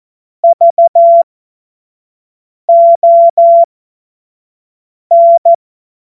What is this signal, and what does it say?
Morse code, heard as a keyed tone, about 14 wpm, spaced as for 5 wpm, 680 Hz. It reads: VON
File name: Morsekurs / Witze / Witz 16. Februar 2021, 25 F70